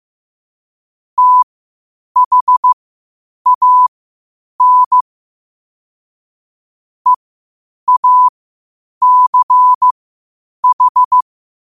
Morse code, heard as a keyed tone, about 15 wpm, spaced as for 8 wpm, 1000 Hz. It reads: THAN EACH
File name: qotdfiles / 8wpm